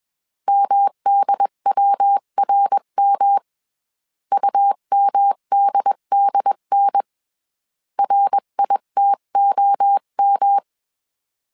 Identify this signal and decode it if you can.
Morse code, heard as a keyed tone, about 21 wpm, 800 Hz. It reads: MDWRM UMDDN RITOM
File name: CW / DET / 1 3 1500C S B 20wpm 1754